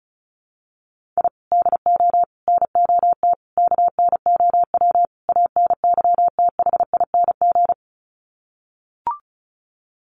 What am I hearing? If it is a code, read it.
Morse code, heard as a keyed tone, about 34 wpm, 700 Hz. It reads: I DO NOT KNOW ANYTHING
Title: Sets of 5 Words Encore 35wpm